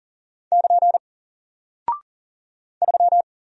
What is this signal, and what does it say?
Morse code, heard as a keyed tone, about 40 wpm, 700 Hz. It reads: <KN> 3
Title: Single Character (Letter:Number:Punctuation:Prosign) Alt 40wpm